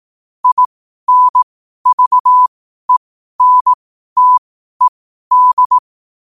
Morse code, heard as a keyed tone, about 18 wpm, spaced as for 12 wpm, 1000 Hz. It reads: INVENTED
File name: qotdfiles / 12wpm